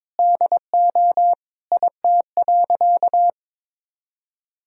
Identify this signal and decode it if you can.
Morse code, heard as a keyed tone, about 22 wpm, 700 Hz. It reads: DO IT.
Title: Sentences from Top 1000 Words - Review Part 2 - 22wpm